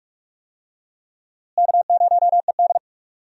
Morse code, heard as a keyed tone, about 47 wpm, 700 Hz. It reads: K0ED
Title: USA Call Signs - 4 Characters - Repeated 3x - 45wpm